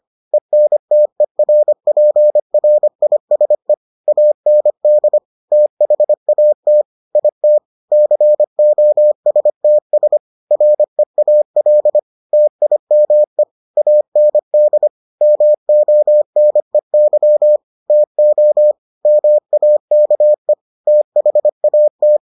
Morse code, heard as a keyed tone, about 25 wpm, 600 Hz. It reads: ENTERPRISE AND THAT IT COSTS REAL TIME AND MONEY TO MAKE THAT